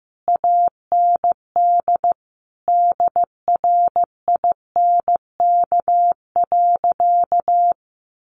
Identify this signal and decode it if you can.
Morse code, heard as a keyed tone, about 15 wpm, 700 Hz. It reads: AND DRINK.
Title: Sentences from Top 1500 Words 15wpm